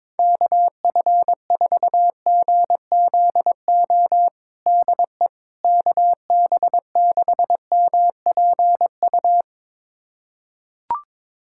Morse code, heard as a keyed tone, about 22 wpm, 700 Hz. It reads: KF4GZO DE KB6MPU
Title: USA Callsign DE Callsign - 6 Characters - 22wpm